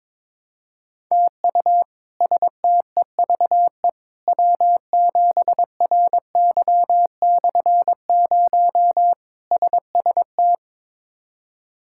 Morse code, heard as a keyed tone, about 22 wpm, 700 Hz. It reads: TU STEVE W7RY/0 SST
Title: K1USN SST QSO Practice - 22wpm